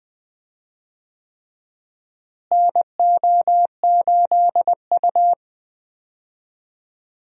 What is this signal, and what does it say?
Morse code, heard as a keyed tone, about 20 wpm, 700 Hz. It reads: NO8U
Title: USA Call Signs - 4 Characters - Repeated 3x - 20wpm